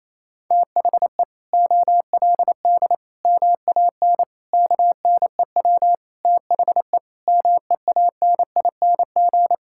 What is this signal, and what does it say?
Morse code, heard as a keyed tone, about 28 wpm, 700 Hz. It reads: THE OLD MAN KNEW THE MEANING
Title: Sentences from Top 1000 Words - Review Part 6 - 28wpm